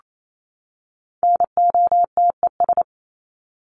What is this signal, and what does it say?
Morse code, heard as a keyed tone, about 28 wpm, 700 Hz. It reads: NOTES
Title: ES:S Suffix - Top 100 Words with Base Words 28wpm